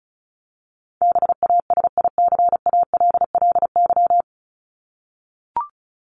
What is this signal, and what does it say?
Morse code, heard as a keyed tone, about 34 wpm, 700 Hz. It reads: BASICALLY
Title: Top 1251-1500 Words 35wpm